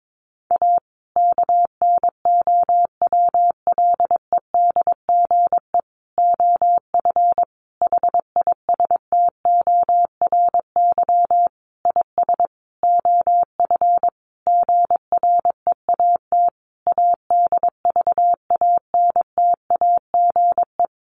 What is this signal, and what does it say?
Morse code, heard as a keyed tone, about 22 wpm, 700 Hz. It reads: A KNOWLEDGE OF HISTORY IS OF GREAT ADVANTAGE
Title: Sentences from Top 900 Words 22wpm